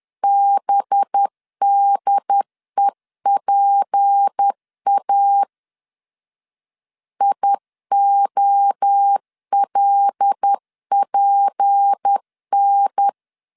Morse code, heard as a keyed tone, about 10 wpm, 800 Hz. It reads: BDEPA IOLPN
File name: CW / DET / 1 6 1500C S C 10wpm 4252